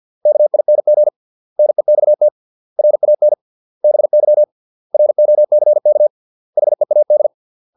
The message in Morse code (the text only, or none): KING NEXT RAN BY ROCK HEAD